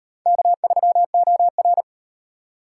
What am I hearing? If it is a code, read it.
Morse code, heard as a keyed tone, about 38 wpm, 700 Hz. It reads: K3OR